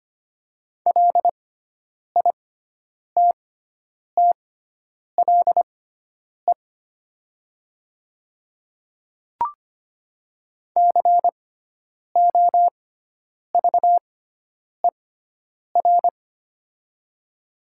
Morse code, heard as a keyed tone, about 25 wpm, spaced as for 9 wpm, 700 Hz. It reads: LITTLE COVER
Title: 6x Character Spacing - Top 300 Words - 25wpm